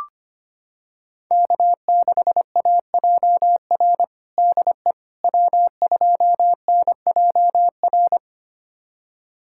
Morse code, heard as a keyed tone, about 25 wpm, 700 Hz. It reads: K6AJR DE W2NJR
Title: USA Callsign DE Callsign - 5 Characters - 25wpm